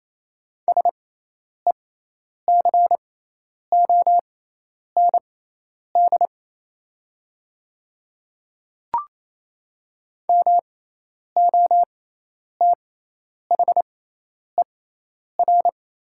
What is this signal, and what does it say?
Morse code, heard as a keyed tone, about 28 wpm, spaced as for 10 wpm, 700 Hz. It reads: SECOND MOTHER
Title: 6x Character Spacing - Top 300 Words - 28wpm